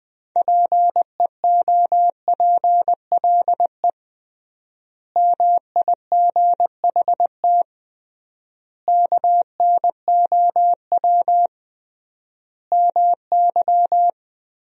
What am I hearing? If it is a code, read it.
Morse code, heard as a keyed tone, about 20 wpm, 700 Hz. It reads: PEOPLE MIGHT KNOW MY
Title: 3x Word-Spacing - Sentences from Top 500 Words - 20wpm